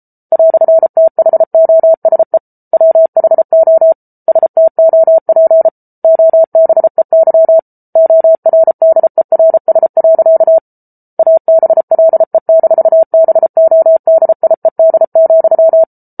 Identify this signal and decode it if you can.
Morse code, heard as a keyed tone, about 34 wpm, 650 Hz. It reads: "THOSE WHO STOP OBEY ORDERS. ABLE-BODIED,